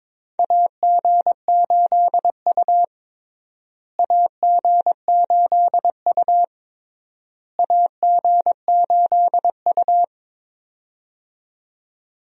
Morse code, heard as a keyed tone, about 21 wpm, 700 Hz. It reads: AG8U AG8U AG8U
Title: USA Call Signs - 4 Characters - Repeated 3x - 22wpm